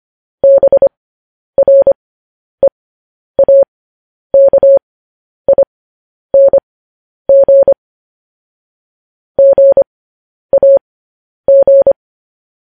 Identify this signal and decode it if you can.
Morse code, heard as a keyed tone, about 25 wpm, spaced as for 10 wpm, 550 Hz. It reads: BREAKING GAG